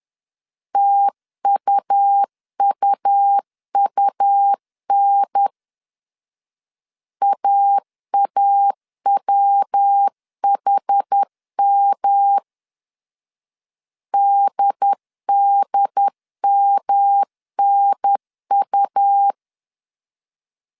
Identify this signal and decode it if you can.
Morse code, heard as a keyed tone, about 10 wpm, 800 Hz. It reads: TUUUN AAWHM DDMNU
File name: CW / DET / 1 3 1500C S B 10wpm 3335